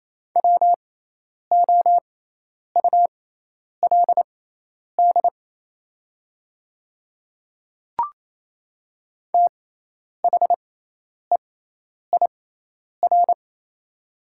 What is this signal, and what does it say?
Morse code, heard as a keyed tone, about 27 wpm, spaced as for 10 wpm, 700 Hz. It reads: WOULD THEIR
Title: 6x Character Spacing - Top 100 Words - 28wpm